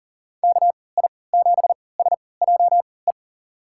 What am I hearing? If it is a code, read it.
Morse code, heard as a keyed tone, about 40 wpm, spaced as for 23 wpm, 700 Hz. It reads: KI7SJE
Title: USA Call Signs - 3x Character Spacing - 40wpm